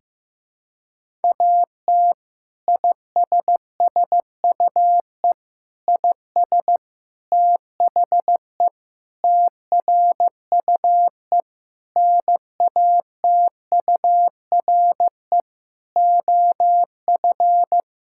Morse code, heard as a keyed tone, about 16 wpm, 700 Hz. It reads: AT ISSUE IS THE TRUE NATURE OF
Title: Sentences from Top 1000 Words - Review Part 6 - 15wpm